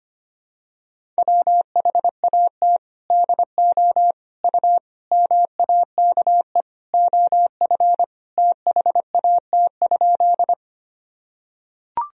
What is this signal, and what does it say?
Morse code, heard as a keyed tone, about 25 wpm, 700 Hz. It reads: WHAT DO U MAKE OF THAT?
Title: Sentences from Top 100 Words and Abbreviations 25wpm